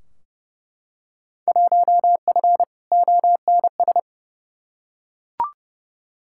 Morse code, heard as a keyed tone, about 30 wpm, 700 Hz. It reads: 1F ONS
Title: Field Day Exchanges Encore - 30wpm